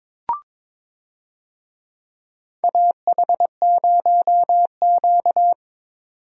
Morse code, heard as a keyed tone, about 22 wpm, 700 Hz. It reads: AH0Q
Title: USA Call Signs - 4 Characters - 22wpm